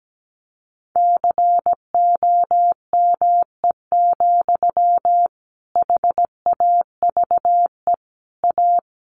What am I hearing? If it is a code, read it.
Morse code, heard as a keyed tone, about 17 wpm, 700 Hz. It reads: COME, HAVE A
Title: Sentences from Top 1500 Words 17wpm